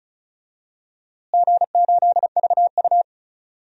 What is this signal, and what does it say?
Morse code, heard as a keyed tone, about 34 wpm, 700 Hz. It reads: G8VU
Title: DXCC-100 Call Signs - 4 Characters - Repeated 2x - 35wpm